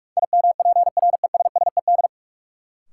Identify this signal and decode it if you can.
Morse code, heard as a keyed tone, about 43 wpm, 700 Hz. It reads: IMPRESSED
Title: Sentences from Top 1600 Words - New Word Drill Alt - 45wpm